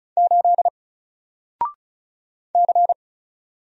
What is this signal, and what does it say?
Morse code, heard as a keyed tone, about 36 wpm, 700 Hz. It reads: 8 C
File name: Single-Letter-Number-35wpm